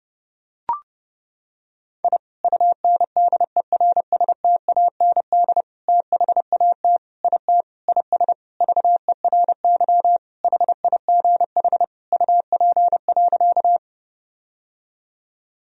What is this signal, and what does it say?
Morse code, heard as a keyed tone, about 31 wpm, 700 Hz. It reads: I UNDERSTAND THAT IT IS VERY HIGH UP.
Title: Sentences from Top 1000 Words - Review Part 1 - 30wpm